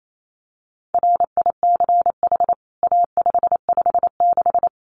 Morse code, heard as a keyed tone, about 28 wpm, 700 Hz. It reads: RICH A556
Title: CWT Exchange 28wpm